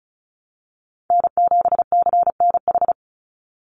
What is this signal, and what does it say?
Morse code, heard as a keyed tone, about 34 wpm, 700 Hz. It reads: N7CNH
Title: NA SOTA Chaser Callsigns 35wpm